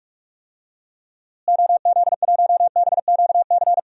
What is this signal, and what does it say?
Morse code, heard as a keyed tone, about 43 wpm, 700 Hz. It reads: OZ1BQC
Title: DXCC-100 Call Signs - 6 Characters - Repeated 2x - 45wpm